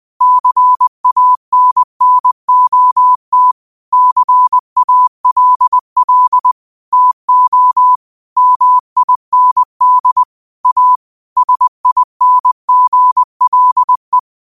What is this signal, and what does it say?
Morse code, heard as a keyed tone, about 20 wpm, 1000 Hz. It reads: CANNOT CALL TO MIND A SINGLE